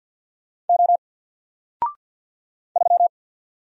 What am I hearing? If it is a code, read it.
Morse code, heard as a keyed tone, about 51 wpm, 700 Hz. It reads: O 3
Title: Single Character (Letter:Number:Punctuation:Prosign) Alt 50wpm